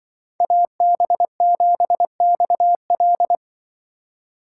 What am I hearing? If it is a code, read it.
Morse code, heard as a keyed tone, about 24 wpm, 700 Hz. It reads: AB7XL
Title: Speed Racing - USA 2x2 Call Signs 20wpm